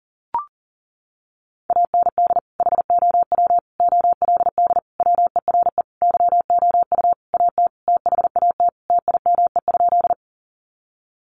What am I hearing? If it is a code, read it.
Morse code, heard as a keyed tone, about 40 wpm, 700 Hz. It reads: AND HOW OLD WERE YOU AT THAT TIME?
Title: Sentences from Top 200 Words 40wpm